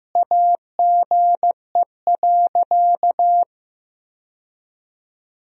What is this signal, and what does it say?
Morse code, heard as a keyed tone, about 15 wpm, 700 Hz. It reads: AGE.
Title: Sentences from Top 1000 Words - Review Part 3 - 15wpm